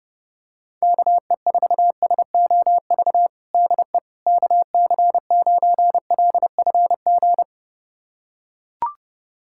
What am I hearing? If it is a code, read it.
Morse code, heard as a keyed tone, about 30 wpm, 700 Hz. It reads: KE4SOV DE KC9LFG
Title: USA Callsign DE Callsign - 6 Characters - 30wpm